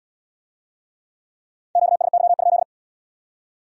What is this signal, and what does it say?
Morse code, heard as a keyed tone, about 66 wpm, 700 Hz. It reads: QSY?